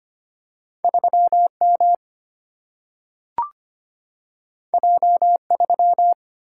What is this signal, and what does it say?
Morse code, heard as a keyed tone, about 25 wpm, 700 Hz. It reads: 3 M J 3
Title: Single-Letter Single-Number 25wpm